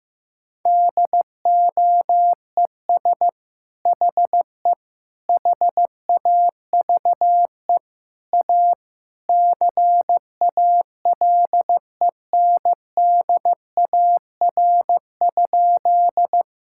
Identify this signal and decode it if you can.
Morse code, heard as a keyed tone, about 15 wpm, 700 Hz. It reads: DOES HE HAVE A CALENDAR?
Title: Sentences from Top 1600 Words 15wpm